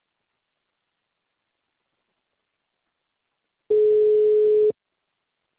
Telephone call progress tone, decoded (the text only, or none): ringback tone